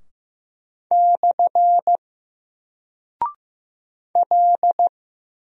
Morse code, heard as a keyed tone, about 15 wpm, 700 Hz. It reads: / L
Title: Single Character (Letter:Number:Punctuation:Prosign) 15wpm